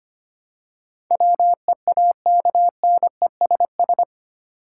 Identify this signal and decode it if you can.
Morse code, heard as a keyed tone, about 25 wpm, 700 Hz. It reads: WEAKNESS
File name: Eight Letter Words - 25wpm